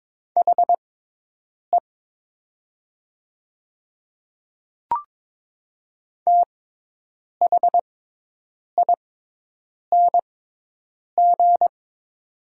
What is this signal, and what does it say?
Morse code, heard as a keyed tone, about 22 wpm, spaced as for 8 wpm, 700 Hz. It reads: HE THING
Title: 6x Character Spacing - Top 100 Words - 22wpm